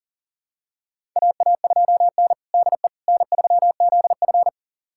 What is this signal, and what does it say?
Morse code, heard as a keyed tone, about 37 wpm, 700 Hz. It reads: AA2N DE N3ZF
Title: USA Callsign DE Callsign - 4 Characters - 40wpm